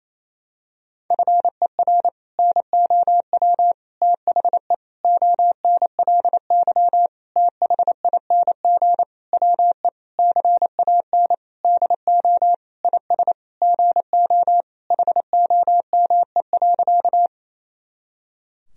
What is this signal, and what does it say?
Morse code, heard as a keyed tone, about 28 wpm, 700 Hz. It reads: FER NOW THE ONLY THING WE CAN DO IS GO HOME.
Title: Sentences from Top 200 Words and Abbreviations 28wpm